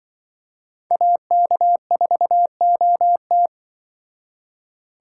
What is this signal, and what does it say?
Morse code, heard as a keyed tone, about 24 wpm, 700 Hz. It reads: AK4OT
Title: Speed Racing - USA 2x2 Call Signs 20wpm